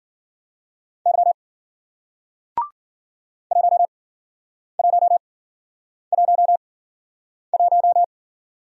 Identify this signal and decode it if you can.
Morse code, heard as a keyed tone, about 56 wpm, spaced as for 9 wpm, 700 Hz. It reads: Y 1111